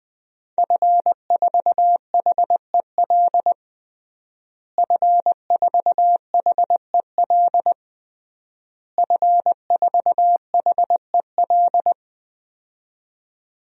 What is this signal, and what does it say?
Morse code, heard as a keyed tone, about 20 wpm, 700 Hz. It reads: F4HEL F4HEL F4HEL
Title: DXCC-100 Call Signs - 5 Characters - Repeated 3x - 20wpm